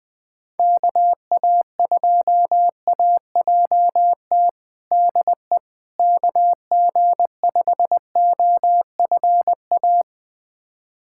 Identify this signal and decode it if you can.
Morse code, heard as a keyed tone, about 20 wpm, 700 Hz. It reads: KA2AJT DE KG5OFA
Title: USA Callsign DE Callsign - 6 Characters - 20wpm